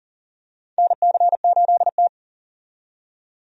Morse code, heard as a keyed tone, about 40 wpm, 700 Hz. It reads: NC8T
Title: USA Call Signs - 4 Characters - 40wpm